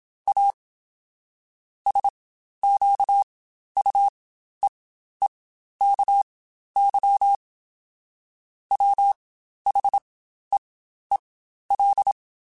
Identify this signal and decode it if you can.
Morse code, heard as a keyed tone, about 27 wpm, spaced as for 12 wpm, 800 Hz. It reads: A SQUEEKY WHEEL
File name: Spaced / Chuckles Spaced-mp3s / Squeaky sp @25